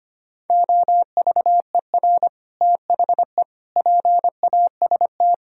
Morse code, heard as a keyed tone, about 31 wpm, 700 Hz. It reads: OVER THE PAST
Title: Speed-Racing - Sets of 3 Words 25wpm